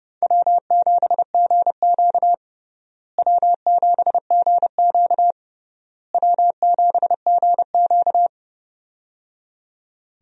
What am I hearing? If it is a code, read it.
Morse code, heard as a keyed tone, about 30 wpm, 700 Hz. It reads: W7GQ W7GQ W7GQ
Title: USA Call Signs - 4 Characters - Repeated 3x - 30wpm